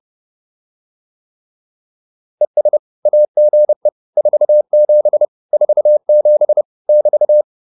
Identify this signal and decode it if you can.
Morse code, heard as a keyed tone, about 30 wpm, 600 Hz. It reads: ES AGE 47 47 <BT>
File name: QSO 404 30